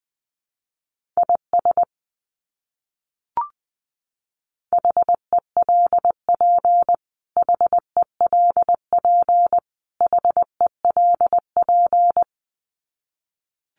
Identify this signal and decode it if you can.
Morse code, heard as a keyed tone, about 20 wpm, 700 Hz. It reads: IS HELP HELP HELP
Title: Top 200 Words Repeated 20wpm